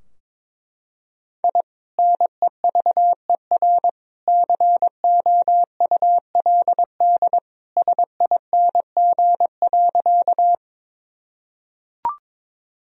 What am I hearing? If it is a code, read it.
Morse code, heard as a keyed tone, about 22 wpm, 700 Hz. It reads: I NEVER COULD SING.
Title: Sentences from Top 1000 Words - Review Part 2 - 22wpm